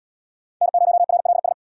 Morse code, heard as a keyed tone, about 64 wpm, 700 Hz. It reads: N0RLS